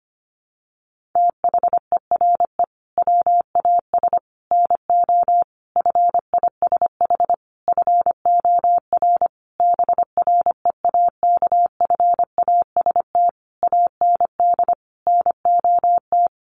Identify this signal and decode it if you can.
Morse code, heard as a keyed tone, about 25 wpm, 700 Hz. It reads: THERE WAS NO FISH FOR BREAKFAST AND NOT
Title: Sentences from Top 1500 Words 25wpm